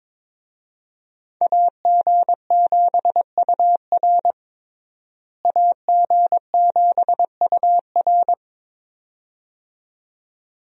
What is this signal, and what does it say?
Morse code, heard as a keyed tone, about 21 wpm, 700 Hz. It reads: AG7UR AG7UR
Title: USA Call Signs - 5 Characters - Repeated 2x - 22wpm